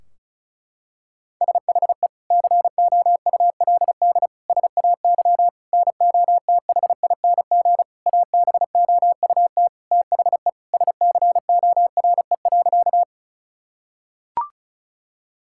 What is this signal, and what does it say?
Morse code, heard as a keyed tone, about 36 wpm, 700 Hz. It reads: SHE COULD SAY NOTHING ABOUT THE SCORE.
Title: Sentences from Top 1600 Words 35wpm